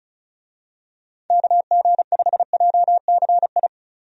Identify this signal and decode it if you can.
Morse code, heard as a keyed tone, about 36 wpm, 700 Hz. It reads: KG5JCI
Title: USA Call Signs - 6 Characters - Repeated 3x - 35wpm